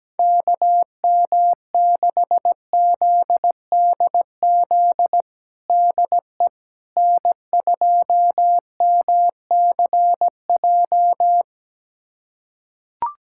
Morse code, heard as a keyed tone, about 17 wpm, 700 Hz. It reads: KM6ZDZ DE N2MCJ
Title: USA Callsign DE Callsign 17wpm